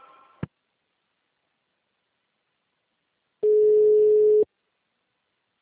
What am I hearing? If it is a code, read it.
Ringback tone (Europe, 425 Hz).
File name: out- -2005- - - .8486